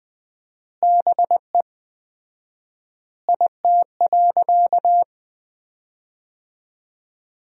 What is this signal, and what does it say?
Morse code, heard as a keyed tone, about 20 wpm, 700 Hz. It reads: BE IT.
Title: 4x Word-Spacing - Sentences from Top 100 Words - 20wpm